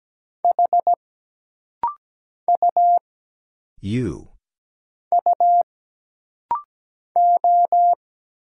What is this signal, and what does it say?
Morse code, heard as a keyed tone, about 17 wpm, 700 Hz. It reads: H U U O
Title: Single Letter-Number Alt - Rapid-Fire 17wpm